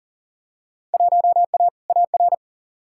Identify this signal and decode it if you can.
Morse code, heard as a keyed tone, about 40 wpm, 700 Hz. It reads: 1A AR
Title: Field Day Exchanges Encore - 40wpm